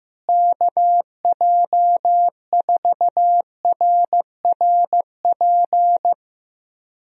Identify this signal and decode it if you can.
Morse code, heard as a keyed tone, about 15 wpm, 700 Hz. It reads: KJ4RRP
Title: USA Call Signs - 6 Characters - Repeated 2x - 15wpm